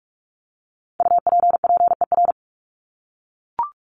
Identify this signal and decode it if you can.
Morse code, heard as a keyed tone, about 44 wpm, 700 Hz. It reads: UPPER
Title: Top 1751-2000 Words 45wpm